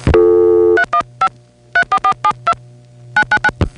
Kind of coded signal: DTMF